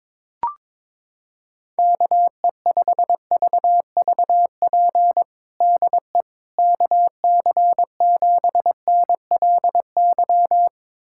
Morse code, heard as a keyed tone, about 22 wpm, 700 Hz. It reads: KE5VVP DE KC7NLY
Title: USA Callsign DE Callsign 22wpm